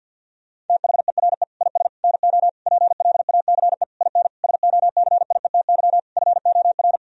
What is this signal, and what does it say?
Morse code, heard as a keyed tone, about 50 wpm, 700 Hz. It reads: THERE IS NO PLACE IN SOCIETY FOR
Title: Sentences from Top 1000 Words - Review Part 4 - 50wpm